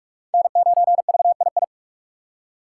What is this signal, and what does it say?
Morse code, heard as a keyed tone, about 42 wpm, 690 Hz. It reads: N9VII
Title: USA Call Signs - 5 Characters - 4x Character Spacing - 45wpm